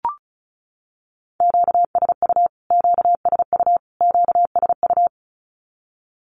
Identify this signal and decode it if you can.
Morse code, heard as a keyed tone, about 34 wpm, 700 Hz. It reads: QSU QSU QSU
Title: Q-Codes Repeated 3x 35wpm